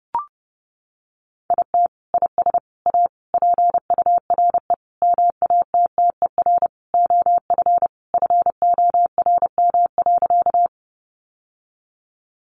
Morse code, heard as a keyed tone, about 30 wpm, 700 Hz. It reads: IT IS A PURE MATTER OF FORM.
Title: Sentences from Top 1500 Words 30wpm